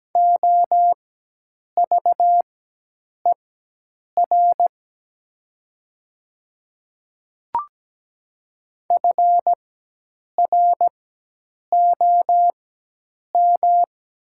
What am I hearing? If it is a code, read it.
Morse code, heard as a keyed tone, about 17 wpm, spaced as for 8 wpm, 700 Hz. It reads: OVER FROM
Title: 4x Character Spacing - Top 500 Words - 17wpm